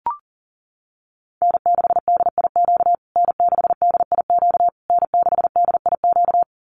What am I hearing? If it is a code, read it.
Morse code, heard as a keyed tone, about 40 wpm, 700 Hz. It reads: N6DIQ N6DIQ N6DIQ
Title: USA Call Signs Repeated 3x 40wpm